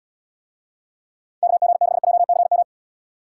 Morse code, heard as a keyed tone, about 74 wpm, 700 Hz. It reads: KG6JCM